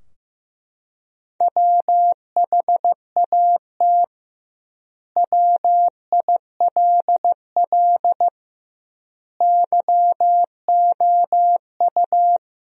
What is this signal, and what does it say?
Morse code, heard as a keyed tone, about 15 wpm, 700 Hz. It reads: WHAT WILL YOU
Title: 2x Word-Spacing - Sentences from Top 100 Words - 15wpm